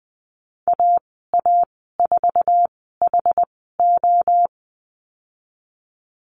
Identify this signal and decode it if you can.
Morse code, heard as a keyed tone, about 20 wpm, spaced as for 15 wpm, 700 Hz. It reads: AA4HO
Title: 076 Lesson 13g - Letter L - Call Signs 20wpm